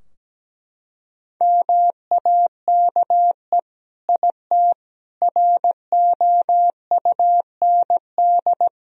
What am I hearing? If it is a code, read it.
Morse code, heard as a keyed tone, about 17 wpm, 700 Hz. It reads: MAKE IT ROUND
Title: Sentences from Top 1000 Words - Review Part 3 - 17wpm